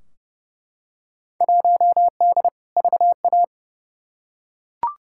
Morse code, heard as a keyed tone, about 32 wpm, 700 Hz. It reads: 1D VA
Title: Field Day Exchanges - 30wpm